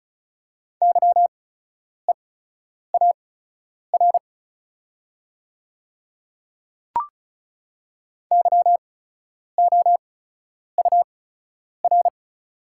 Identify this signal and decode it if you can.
Morse code, heard as a keyed tone, about 34 wpm, spaced as for 10 wpm, 690 Hz. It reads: YEAR YOUR
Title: 8x Character Spacing - Top 200 Words - 35wpm